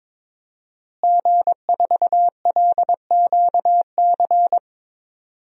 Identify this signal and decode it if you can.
Morse code, heard as a keyed tone, about 22 wpm, 700 Hz. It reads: G4LQC
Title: DXCC-100 Call Signs - 5 Characters - Repeated 2x - 22wpm